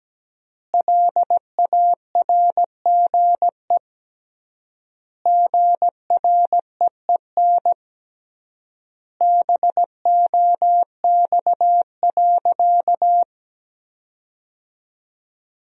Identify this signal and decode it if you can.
Morse code, heard as a keyed tone, about 17 wpm, 700 Hz. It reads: LARGE GREEN BOX.
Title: 3x Word-Spacing - Sentences from Top 500 Words - 17wpm